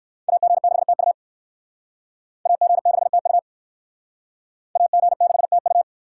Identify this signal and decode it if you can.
Morse code, heard as a keyed tone, about 58 wpm, 690 Hz. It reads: AG6TU AG6TU AG6TU